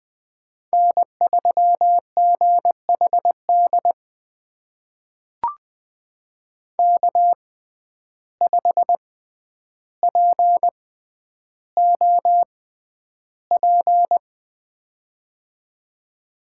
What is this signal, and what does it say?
Morse code, heard as a keyed tone, about 20 wpm, spaced as for 10 wpm, 700 Hz. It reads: N3GHD K5POP
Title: USA Call Signs - 5 Characters - 6x Character Spacing - 20wpm